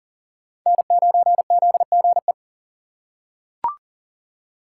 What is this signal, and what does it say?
Morse code, heard as a keyed tone, about 38 wpm, 700 Hz. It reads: N9ZGE
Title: USA Call Signs - 5 Characters - 3x Character Spacing - 40wpm